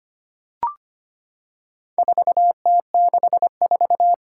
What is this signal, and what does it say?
Morse code, heard as a keyed tone, about 25 wpm, 700 Hz. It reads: 4T64
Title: Four Challenging Letter-Number Combinations 25wpm